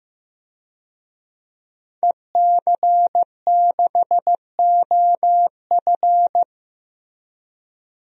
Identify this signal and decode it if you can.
Morse code, heard as a keyed tone, about 15 wpm, 700 Hz. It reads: EC6OF
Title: DXCC-100 Call Signs - 5 Characters - Repeated 2x - 15wpm